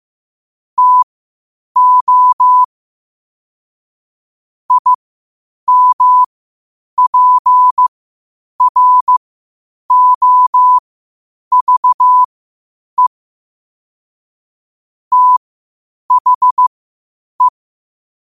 Morse code, heard as a keyed tone, about 15 wpm, spaced as for 8 wpm, 1000 Hz. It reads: TO IMPROVE THE
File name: qotdfiles / 8wpm